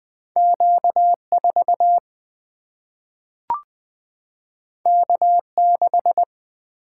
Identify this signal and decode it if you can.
Morse code, heard as a keyed tone, about 20 wpm, 700 Hz. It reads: Q4 K6
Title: Single-Letter Single-Number 20wpm